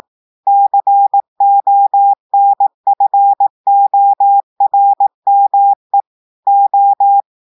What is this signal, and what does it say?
Morse code, heard as a keyed tone, about 18 wpm, 800 Hz. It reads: CONFORME O
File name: audios / portugais / portugais texte en clair 2 vitesse 18 mots minutes